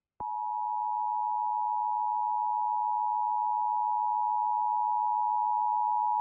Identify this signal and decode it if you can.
Emergency Alert System (SAME) tones (the 853 and 960 Hz attention tone). It is heard from 0.5 to 5.9 s.